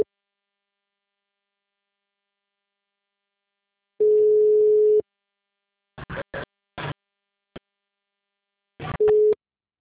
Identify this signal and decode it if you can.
Ringback tone (Europe, 425 Hz).
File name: out- -2008- - - .32407